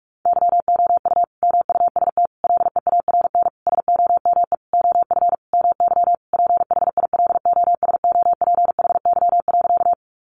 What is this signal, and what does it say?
Morse code, heard as a keyed tone, about 44 wpm, 700 Hz. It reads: YOU MUST LEARN SOME OF MY PHILOSOPHY.
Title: Sentences from Top 900 Words 45wpm